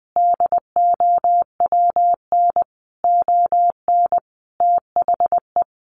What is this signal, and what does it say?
Morse code, heard as a keyed tone, about 20 wpm, 700 Hz. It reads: DOWN ON THE
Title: Sentences from Top 1200 Words 20wpm